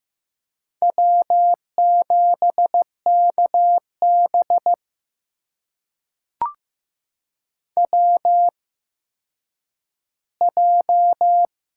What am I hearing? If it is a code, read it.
Morse code, heard as a keyed tone, about 15 wpm, spaced as for 9 wpm, 700 Hz. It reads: W7KB WJ
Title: USA Call Signs - 4 Characters - 8x Character Spacing - 15wpm